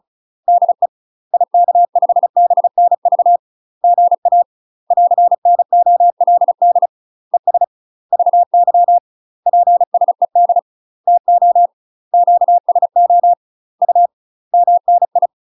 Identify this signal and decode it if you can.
Morse code, heard as a keyed tone, about 36 wpm, 700 Hz. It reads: DE IK5BNV GA <AR>NOLD ES VY PSED TO QSO U MNI
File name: UK MASTER QSOs TRAINING / 35WPM / UK MASTER QSOs 0041